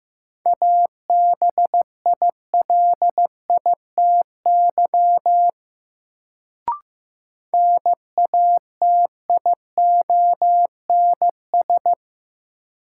Morse code, heard as a keyed tone, about 15 wpm, 700 Hz. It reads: ABILITY NATIONS
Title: Sets of 3 Words - New Word Drill - 15wpm